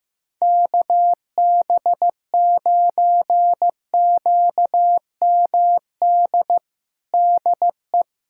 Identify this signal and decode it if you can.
Morse code, heard as a keyed tone, about 16 wpm, 700 Hz. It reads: KB9QMD DE
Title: USA Callsign DE Callsign 15wpm